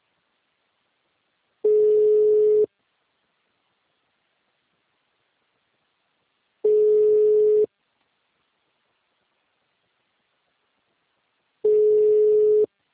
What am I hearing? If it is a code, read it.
Ringback tone (Europe, 425 Hz).
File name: out- -2005- - - .170